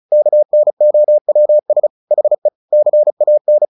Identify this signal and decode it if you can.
Morse code, heard as a keyed tone, about 34 wpm, 600 Hz. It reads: KNOWS HE CAN
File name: the old man and the sea / the old man and the sea 1 35wpm